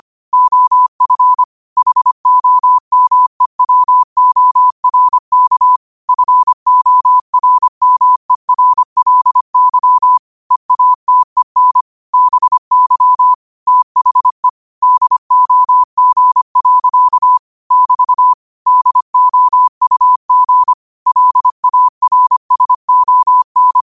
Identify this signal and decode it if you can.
Morse code, heard as a keyed tone, about 25 wpm, 1000 Hz. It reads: OF HOMEWORK FORMERLY EATEN BY THE DOG. <BT> DOUG LARSON
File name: qotdfiles / 25wpm